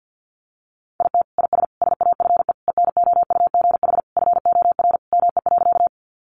Morse code, heard as a keyed tone, about 50 wpm, 700 Hz. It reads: IT IS SAFE ENOUGH FOR ME.
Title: Sentences from Top 900 Words 50wpm